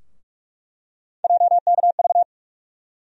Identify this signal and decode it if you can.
Morse code, heard as a keyed tone, about 43 wpm, 700 Hz. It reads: JKV
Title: USA Call Sign Postfixes - 3 Letters 45wpm